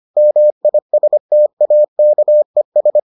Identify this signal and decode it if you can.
Morse code, heard as a keyed tone, about 25 wpm, 600 Hz. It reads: MISTAKES